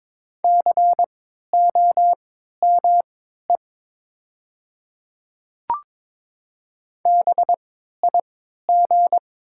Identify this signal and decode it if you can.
Morse code, heard as a keyed tone, about 22 wpm, spaced as for 13 wpm, 700 Hz. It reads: COME BIG
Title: 3x Character Spacing - Top 500 Words - 22wpm